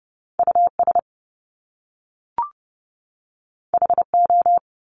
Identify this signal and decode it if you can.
Morse code, heard as a keyed tone, about 30 wpm, 700 Hz. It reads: US HO
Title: Top Two Letter Combinations in Words 30wpm